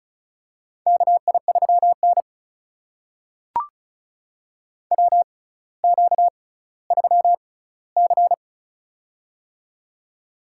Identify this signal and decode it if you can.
Morse code, heard as a keyed tone, about 36 wpm, spaced as for 18 wpm, 700 Hz. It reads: KI3N WQ3C
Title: USA Call Signs - 4 Characters - 6x Character Spacing - 35wpm